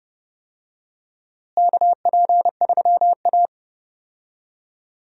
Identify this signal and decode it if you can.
Morse code, heard as a keyed tone, about 30 wpm, 700 Hz. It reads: KP3A
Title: USA Call Signs - 4 Characters - Repeated 2x - 30wpm